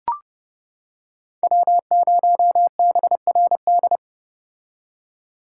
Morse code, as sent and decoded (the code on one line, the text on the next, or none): .-- ----- -... .-. -..
W0BRD